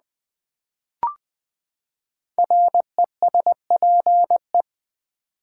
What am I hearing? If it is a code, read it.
Morse code, heard as a keyed tone, about 20 wpm, 700 Hz. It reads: RESPE
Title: Build A Word - 5 Letters At A Time - 12-Letter Words - 20wpm